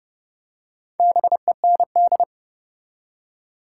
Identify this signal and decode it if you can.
Morse code, heard as a keyed tone, about 30 wpm, 700 Hz. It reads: BEND
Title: Four Letter Words - 30wpm